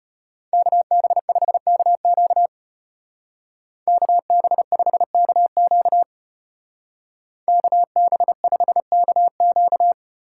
Morse code, heard as a keyed tone, about 30 wpm, 700 Hz. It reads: KB5KQ KB5KQ KB5KQ
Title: Speed Racing - USA 2x2 Call Signs 25wpm